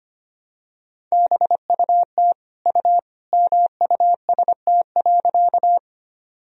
Morse code, heard as a keyed tone, about 25 wpm, 700 Hz. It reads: BUT U MUST.
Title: Sentences from Top 200 Words and Abbreviations 25wpm